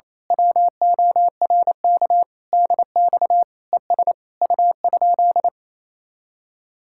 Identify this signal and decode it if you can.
Morse code, heard as a keyed tone, about 26 wpm, 700 Hz. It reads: WORK DX ES U?
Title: Speed-Racing Common QSO Phrases - 28wpm